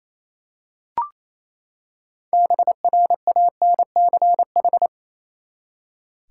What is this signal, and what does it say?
Morse code, heard as a keyed tone, about 28 wpm, 700 Hz. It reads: BRANCH
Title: Six Letter Words - 28wpm